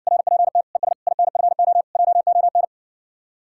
Morse code, heard as a keyed tone, about 60 wpm, 700 Hz. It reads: RPT ES INFO JON